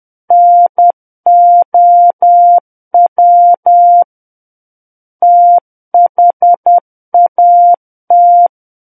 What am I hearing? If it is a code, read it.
Morse code, heard as a keyed tone, about 10 wpm, 700 Hz. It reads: NOW THAT
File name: Book One / 01-Chapter 1 - 10 WPM